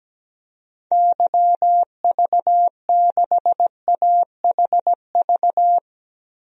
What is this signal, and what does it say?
Morse code, heard as a keyed tone, about 17 wpm, 700 Hz. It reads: YV6AHV
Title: DXCC-100 Call Signs - 6 Characters - Repeated 2x - 17wpm